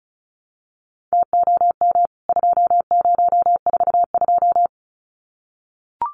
Morse code, heard as a keyed tone, about 34 wpm, 700 Hz. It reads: TOM 2042
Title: CWT Exchange 35wpm